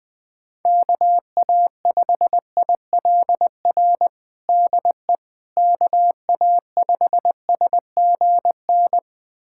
Morse code, heard as a keyed tone, about 20 wpm, 700 Hz. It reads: KA5ILR DE KA5SGN
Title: USA Callsign DE Callsign - 6 Characters - 20wpm